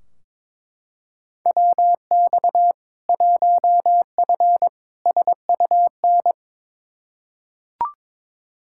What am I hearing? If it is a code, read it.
Morse code, heard as a keyed tone, about 22 wpm, 700 Hz. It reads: WX 1F SUN
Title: Practice QSO - Exchange 2 - Info - 22wpm